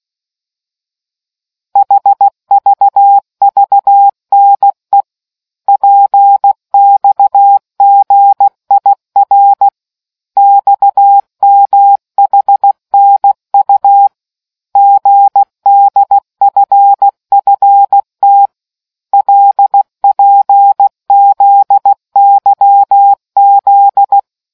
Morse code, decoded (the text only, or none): HVVNE PXGIR XMHNU GDFFT LPZYZ